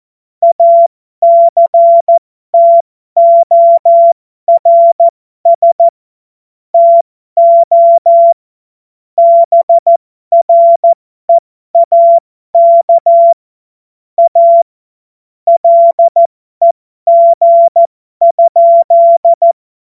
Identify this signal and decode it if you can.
Morse code, heard as a keyed tone, about 14 wpm, spaced as for 12 wpm, 680 Hz. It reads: ACTORS TO BREAK A LEG?
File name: Morsekurs / Witze